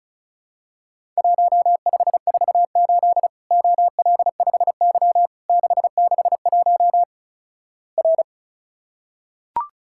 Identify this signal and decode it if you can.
Morse code, heard as a keyed tone, about 36 wpm, 700 Hz. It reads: 1548 OL5Y 661 R
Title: Worked All Europe DX Contest - QTC Exchange - 35wpm